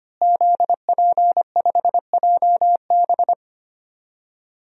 Morse code, heard as a keyed tone, about 25 wpm, 700 Hz. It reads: ZP5JB
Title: DXCC-100 Call Signs - 5 Characters - Repeated 2x - 25wpm